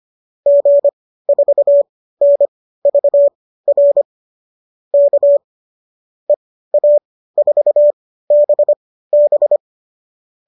Morse code, heard as a keyed tone, about 25 wpm, spaced as for 15 wpm, 570 Hz. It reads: G4NVR K EA4BB